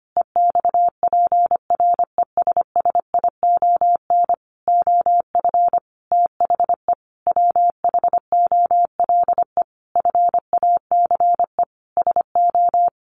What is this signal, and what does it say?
Morse code, heard as a keyed tone, about 25 wpm, 700 Hz. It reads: EXPRESSION OF THE WHOLE FACE SO
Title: Sentences from Top 1400 Words 25wpm